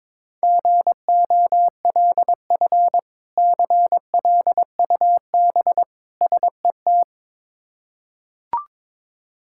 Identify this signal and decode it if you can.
Morse code, heard as a keyed tone, about 22 wpm, 700 Hz. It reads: GOLF CLUB SET
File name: Difficult Words with Hints - 22wpm